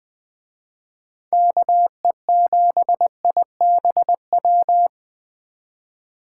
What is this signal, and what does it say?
Morse code, heard as a keyed tone, about 20 wpm, 700 Hz. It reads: KE7IBW
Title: Speed Racing - USA 2x3 Call Signs 17wpm